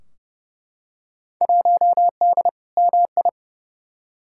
Morse code, heard as a keyed tone, about 30 wpm, 700 Hz. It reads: 1D MI
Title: Field Day Exchanges - 30wpm